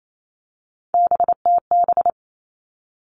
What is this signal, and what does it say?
Morse code, heard as a keyed tone, about 28 wpm, 700 Hz. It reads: BTB
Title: All Cut Numbers - 3-Digit 28wpm